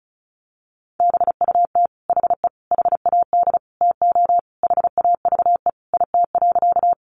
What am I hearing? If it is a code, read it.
Morse code, heard as a keyed tone, about 34 wpm, 700 Hz. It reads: BUT HE HAD TO HAVE IT.
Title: Sentences from Top 100 Words 35wpm